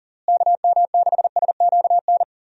Morse code, heard as a keyed tone, about 40 wpm, 700 Hz. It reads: KM6SQN
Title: USA Call Signs - 6 Characters - Repeated 2x - 40wpm